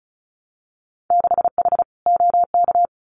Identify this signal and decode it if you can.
Morse code, heard as a keyed tone, about 34 wpm, 700 Hz. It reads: 6H OK
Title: Winter Field Day Exchanges Encore 35wpm